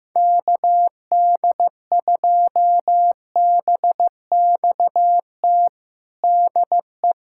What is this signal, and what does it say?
Morse code, heard as a keyed tone, about 16 wpm, 700 Hz. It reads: KD2BXT DE
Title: USA Callsign DE Callsign 15wpm